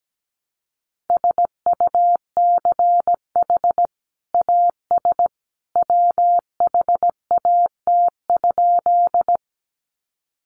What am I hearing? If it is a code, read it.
Morse code, heard as a keyed tone, about 17 wpm, 700 Hz. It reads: SUCH AS WHAT?
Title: Sentences from Top 200 Words 17wpm